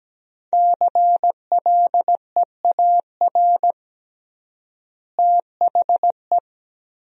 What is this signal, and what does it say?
Morse code, heard as a keyed tone, about 17 wpm, 700 Hz. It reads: CLEAR THE
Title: 3x Word-Spacing - Sentences from Top 500 Words - 17wpm